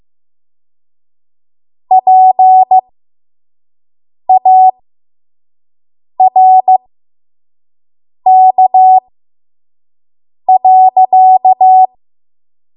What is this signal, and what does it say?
Morse code, heard as a keyed tone, about 15 wpm, spaced as for 5 wpm, 750 Hz. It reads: PARK.
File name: Archive / 5 WPM / 05WPM